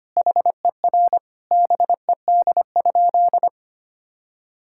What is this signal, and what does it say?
Morse code, heard as a keyed tone, about 25 wpm, 700 Hz. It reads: HER BED?
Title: Sentences from Top 1000 Words - Review Part 6 - 25wpm